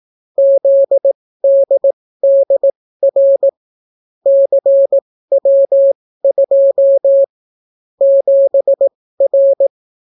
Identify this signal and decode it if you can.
Morse code, heard as a keyed tone, about 18 wpm, spaced as for 14 wpm, 550 Hz. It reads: ZDDR CW2 7R